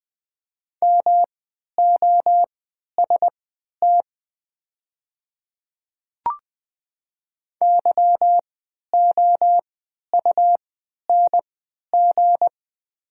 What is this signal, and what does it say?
Morse code, heard as a keyed tone, about 20 wpm, spaced as for 11 wpm, 690 Hz. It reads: MOST YOUNG
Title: 3x Character Spacing - Top 500 Words - 20wpm